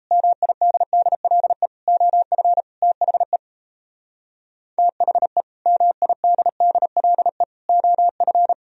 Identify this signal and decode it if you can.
Morse code, heard as a keyed tone, about 35 wpm, 700 Hz. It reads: MIDDLE OF THE THE MIDDLE OF
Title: Speed-Racing - Sets of 4 Words 28wpm